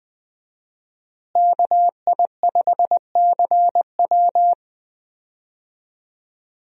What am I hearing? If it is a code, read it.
Morse code, heard as a keyed tone, about 20 wpm, 700 Hz. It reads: KI5CW
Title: USA Call Signs - 5 Characters - Repeated 2x - 20wpm